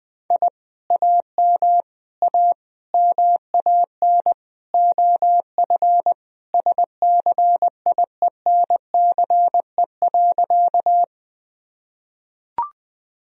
Morse code, heard as a keyed tone, about 21 wpm, 700 Hz. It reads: I AM A MAN OF SCIENCE.
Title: Sentences from Top 1000 Words - Review Part 2 - 20wpm